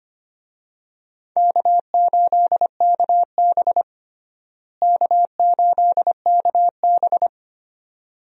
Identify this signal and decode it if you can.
Morse code, heard as a keyed tone, about 25 wpm, 700 Hz. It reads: K8KB K8KB
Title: USA Call Signs - 4 Characters - Repeated 2x - 25wpm